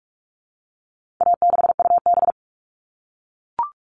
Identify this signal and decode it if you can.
Morse code, heard as a keyed tone, about 44 wpm, 700 Hz. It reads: A6UB
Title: All Cut Numbers - 4-Digit 45wpm